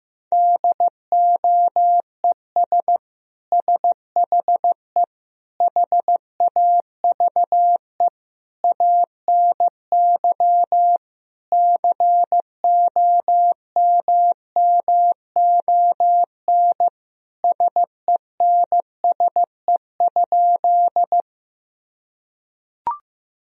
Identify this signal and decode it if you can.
Morse code, heard as a keyed tone, about 15 wpm, 700 Hz. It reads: DOES SHE HAVE ANY COMMON SENSE?
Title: Sentences from Top 1000 Words - Review Part 4 - 15wpm